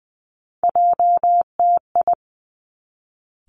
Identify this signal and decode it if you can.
Morse code, heard as a keyed tone, about 20 wpm, 700 Hz. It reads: JTI
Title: Three Challenging Letter-Number Combinations 20wpm